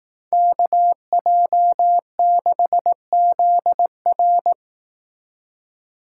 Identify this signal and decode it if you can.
Morse code, heard as a keyed tone, about 18 wpm, 700 Hz. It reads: KJ6ZR
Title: Speed Racing - USA Call Signs 15wpm